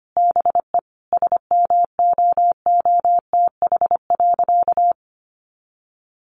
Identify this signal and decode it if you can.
Morse code, heard as a keyed tone, about 25 wpm, 700 Hz. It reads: BE SMOOTH.
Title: Sentences from Top 1400 Words 25wpm